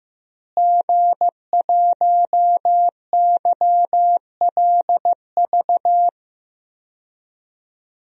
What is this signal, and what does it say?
Morse code, heard as a keyed tone, about 15 wpm, 700 Hz. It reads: G1YLV
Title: DXCC-100 Call Signs - 5 Characters - Repeated 3x - 15wpm